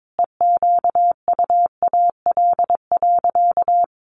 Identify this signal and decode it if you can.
Morse code, heard as a keyed tone, about 22 wpm, 700 Hz. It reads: EQUAL.
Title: Sentences from Top 1500 Words 22wpm